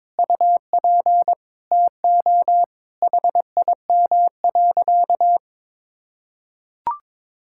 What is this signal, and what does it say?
Morse code, heard as a keyed tone, about 22 wpm, 700 Hz. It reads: UP TO HIM.
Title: Sentences from Top 1000 Words - Review Part 6 - 22wpm